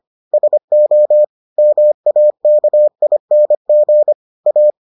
Morse code, heard as a keyed tone, about 25 wpm, 600 Hz. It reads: SO MAKING A